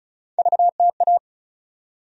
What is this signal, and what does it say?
Morse code, heard as a keyed tone, about 36 wpm, 700 Hz. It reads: VTA
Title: USA Call Sign Postfixes - 3 Letters 35wpm